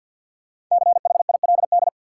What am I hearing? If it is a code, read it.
Morse code, heard as a keyed tone, about 53 wpm, 700 Hz. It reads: CHILD
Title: Sets of 5 Words Encore - New Word Drill - 50wpm